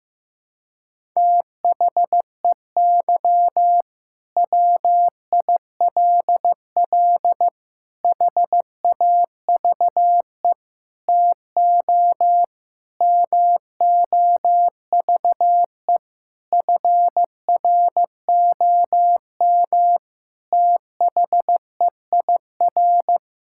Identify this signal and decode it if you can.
Morse code, heard as a keyed tone, about 15 wpm, 700 Hz. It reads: THEY WILL HAVE TO MOVE FROM THEIR
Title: Sentences from Top 1000 Words - Review Part 6 - 15wpm